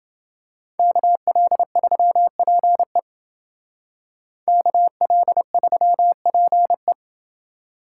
Morse code, heard as a keyed tone, about 29 wpm, 700 Hz. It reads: KL3PE KL3PE